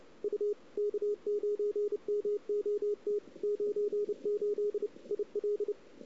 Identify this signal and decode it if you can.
Morse code, heard as a keyed tone, about 33 wpm, 410 Hz, with radio noise. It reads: U K9MOT 98 IL
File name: CW / 2008-ss-cw-k5tr-op-n5rz -00:00